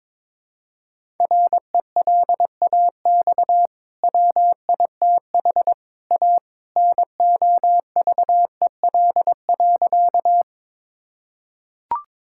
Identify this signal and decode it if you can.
Morse code, heard as a keyed tone, about 22 wpm, 700 Hz. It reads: RELAX WITH A NOVEL.
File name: practice / 22wpm / Easy Sentences with Hints - 22wpm